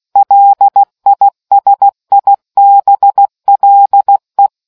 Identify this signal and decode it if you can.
Morse code, heard as a keyed tone, about 16 wpm, 800 Hz. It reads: LISIBLE